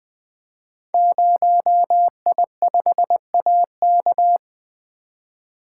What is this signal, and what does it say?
Morse code, heard as a keyed tone, about 20 wpm, 700 Hz. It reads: 0I5AK
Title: Code Groups - 5 Letter:Numbers - 20wpm